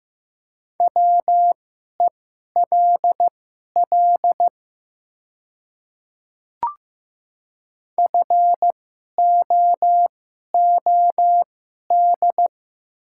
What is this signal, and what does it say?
Morse code, heard as a keyed tone, about 15 wpm, spaced as for 11 wpm, 700 Hz. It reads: WELL FOOD
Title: 2x Character Spacing - Top 300 Words - 15wpm